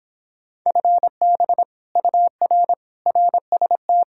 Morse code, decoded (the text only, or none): FB UR RST